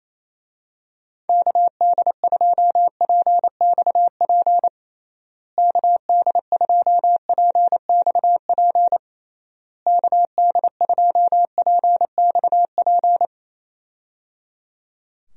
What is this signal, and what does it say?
Morse code, heard as a keyed tone, about 28 wpm, 700 Hz. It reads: KD2PXP KD2PXP KD2PXP
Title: USA Call Signs - 6 Characters - Repeated 3x - 28wpm